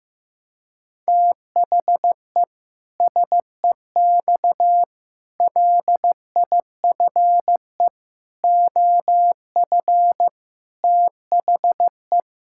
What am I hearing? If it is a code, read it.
Morse code, heard as a keyed tone, about 15 wpm, 700 Hz. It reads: THE SEX LIFE OF THE
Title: Pangram Challenge v2 - 15wpm